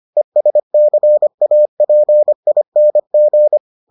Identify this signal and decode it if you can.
Morse code, heard as a keyed tone, about 25 wpm, 600 Hz. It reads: ESCAPING